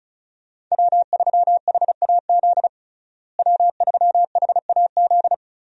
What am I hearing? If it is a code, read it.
Morse code, heard as a keyed tone, about 36 wpm, 700 Hz. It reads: W3HAZ W3HAZ
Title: USA Call Signs - 5 Characters - Repeated 3x - 35wpm